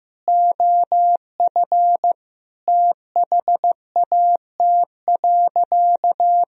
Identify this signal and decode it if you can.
Morse code, heard as a keyed tone, about 15 wpm, 700 Hz. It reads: OF THAT.
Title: Sentences from Top 1000 Words - Review Part 2 - 15wpm